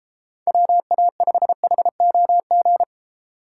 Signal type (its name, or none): Morse code